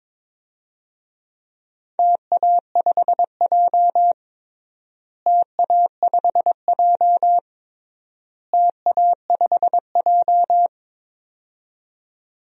Morse code, heard as a keyed tone, about 22 wpm, 700 Hz. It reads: TA5J TA5J TA5J
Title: DXCC-100 Call Signs - 4 Characters - Repeated 3x - 22wpm